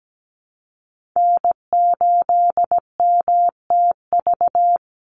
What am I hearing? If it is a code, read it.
Morse code, heard as a keyed tone, about 17 wpm, 700 Hz. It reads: N8MTV
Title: Call Signs Repeated 3x - 17wpm